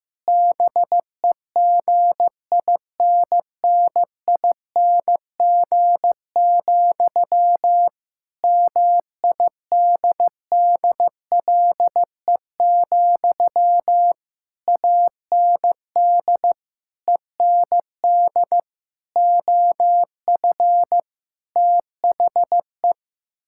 Morse code, heard as a keyed tone, about 15 wpm, 700 Hz. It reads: BEGINNING, MIDDLE, AND END OF THE
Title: Sentences from Top 1000 Words - Review Part 6 - 15wpm